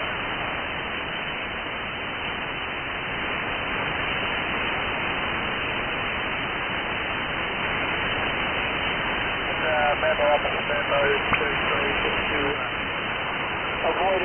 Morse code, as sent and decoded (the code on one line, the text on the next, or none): none